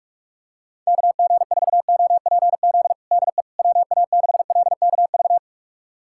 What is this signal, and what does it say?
Morse code, heard as a keyed tone, about 46 wpm, 700 Hz. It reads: KG4OPZ DE WA6LKV
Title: USA Callsign DE Callsign - 6 Characters - 45wpm